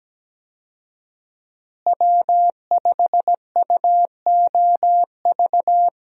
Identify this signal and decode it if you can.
Morse code, heard as a keyed tone, about 17 wpm, 700 Hz. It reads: W5UOV
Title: USA Call Signs - 5 Characters - Repeated 2x - 17wpm